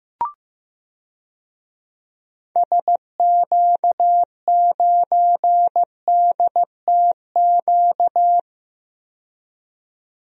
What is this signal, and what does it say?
Morse code, heard as a keyed tone, about 15 wpm, 700 Hz. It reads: SQ9DTQ
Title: DXCC-100 Call Signs - 6 Characters - 15wpm